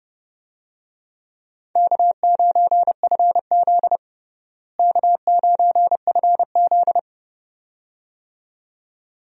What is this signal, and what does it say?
Morse code, heard as a keyed tone, about 30 wpm, 700 Hz. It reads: K9FZ K9FZ
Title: USA Call Signs - 4 Characters - Repeated 2x - 30wpm